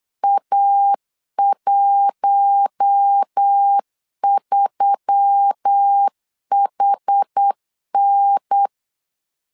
Morse code, heard as a keyed tone, about 8 wpm, 800 Hz. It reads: A13HN
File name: LEZIONI / 7 GF9Z / 7 GF9 8wpm 3545